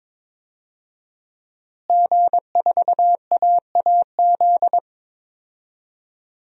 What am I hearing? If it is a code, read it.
Morse code, heard as a keyed tone, about 22 wpm, 700 Hz. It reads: G4AAZ
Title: DXCC-100 Call Signs - 5 Characters - Repeated 2x - 22wpm